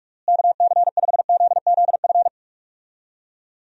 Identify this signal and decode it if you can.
Morse code, heard as a keyed tone, about 46 wpm, 700 Hz. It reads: KC5ZZF
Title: USA Call Signs - 6 Characters - Repeated 3x - 45wpm